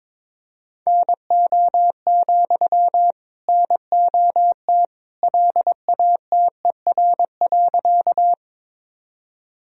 Morse code, heard as a keyed tone, about 22 wpm, 700 Hz. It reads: NO, NOT LATER.
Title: Sentences from Top 1000 Words - Review Part 4 - 22wpm